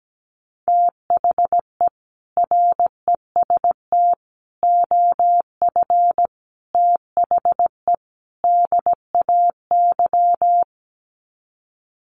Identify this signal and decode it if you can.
Morse code, heard as a keyed tone, about 17 wpm, 700 Hz. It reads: THE REST OF THE DAY
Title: Sets of 5 Words 17wpm